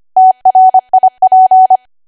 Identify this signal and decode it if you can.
Morse code, heard as a keyed tone, about 25 wpm, 750 Hz. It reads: TRIP